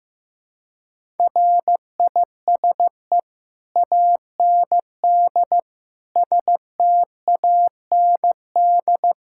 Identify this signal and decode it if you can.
Morse code, heard as a keyed tone, about 15 wpm, 700 Hz. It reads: RISE AND STAND
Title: Sentences from Top 1600 Words 15wpm